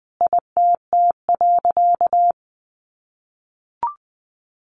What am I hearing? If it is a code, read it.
Morse code, heard as a keyed tone, about 20 wpm, 700 Hz. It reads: ITT.
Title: A Sound Of Thunder 20wpm